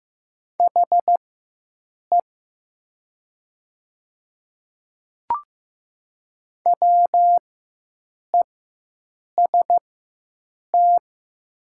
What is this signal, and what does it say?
Morse code, heard as a keyed tone, about 15 wpm, spaced as for 7 wpm, 700 Hz. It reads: HE WEST
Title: 4x Character Spacing - Top 500 Words - 15wpm